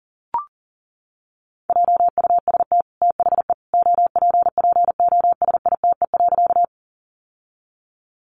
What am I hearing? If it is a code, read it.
Morse code, heard as a keyed tone, about 40 wpm, 700 Hz. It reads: JUST THE OPPOSITE.
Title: Sentences from Top 1300 Words 40wpm